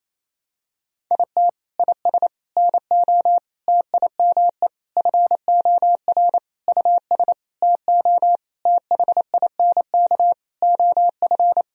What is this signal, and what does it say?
Morse code, heard as a keyed tone, about 28 wpm, 700 Hz. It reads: IT IS NO TIME FOR US TO THINK OF
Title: Sentences from Top 1000 Words - Review Part 3 - 28wpm